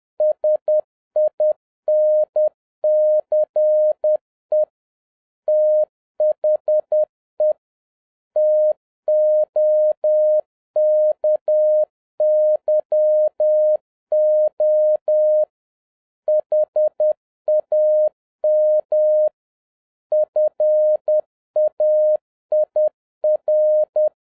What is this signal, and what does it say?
Morse code, heard as a keyed tone, about 10 wpm, 610 Hz. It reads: SINCE THE TOKYO HAM FAIR
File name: 2011-07-mp3 / n46 JOHN K3TN 10wpm